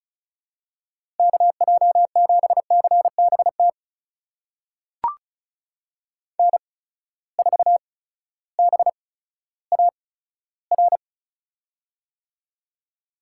Morse code, heard as a keyed tone, about 36 wpm, spaced as for 16 wpm, 700 Hz. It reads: KJ7CBT N4BAR
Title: USA Call Signs - 8x Character Spacing - 35wpm